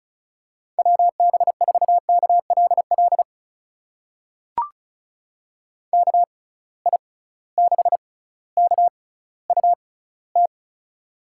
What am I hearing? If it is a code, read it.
Morse code, heard as a keyed tone, about 36 wpm, spaced as for 18 wpm, 700 Hz. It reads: WB4KLL KI6KUT
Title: USA Call Signs - 6 Characters - 6x Character Spacing - 35wpm